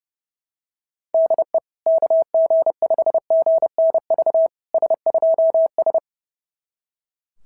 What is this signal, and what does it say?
Morse code, heard as a keyed tone, about 30 wpm, 660 Hz. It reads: DE KG5GNV S2S
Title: SOTA QSO Practice 30wpm